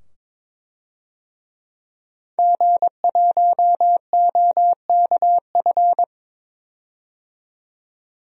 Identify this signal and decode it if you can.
Morse code, heard as a keyed tone, about 22 wpm, 700 Hz. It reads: G1OKF
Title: DXCC-100 Call Signs - 5 Characters - 22wpm